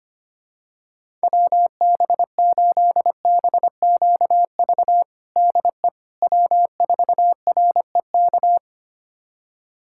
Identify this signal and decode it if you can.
Morse code, heard as a keyed tone, about 25 wpm, 700 Hz. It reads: WB8BQV DE W4REK
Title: USA Callsign DE Callsign 25wpm